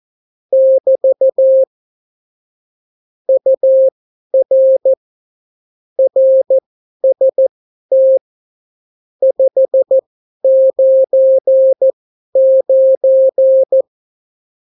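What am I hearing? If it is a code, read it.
Morse code, heard as a keyed tone, about 14 wpm, spaced as for 11 wpm, 530 Hz. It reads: <BT> UR RST 599
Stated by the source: Example Spanish CW QSO from a Morse learning book, 14 wpm characters with 8 wpm spacing.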